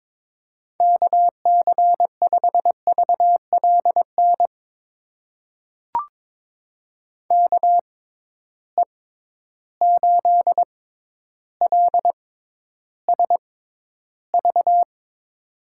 Morse code, heard as a keyed tone, about 22 wpm, spaced as for 11 wpm, 700 Hz. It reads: KC5VLN KE8LSV
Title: USA Call Signs - 6 Characters - 6x Character Spacing - 22wpm